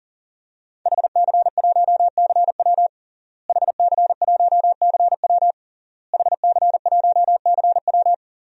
Morse code, heard as a keyed tone, about 37 wpm, 700 Hz. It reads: HC1CW HC1CW HC1CW
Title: DXCC-100 Call Signs - 5 Characters - Repeated 3x - 40wpm